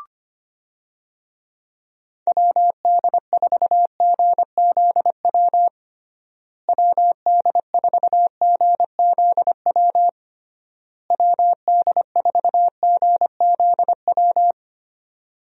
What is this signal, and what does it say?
Morse code, heard as a keyed tone, about 25 wpm, 700 Hz. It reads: WD4GZW WD4GZW WD4GZW
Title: USA Call Signs - 6 Characters - Repeated 3x - 25wpm